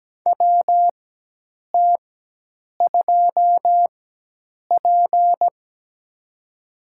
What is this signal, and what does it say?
Morse code, heard as a keyed tone, about 17 wpm, spaced as for 8 wpm, 700 Hz. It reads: WT2P
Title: USA Call Signs - 4 Characters - 4x Character Spacing - 17wpm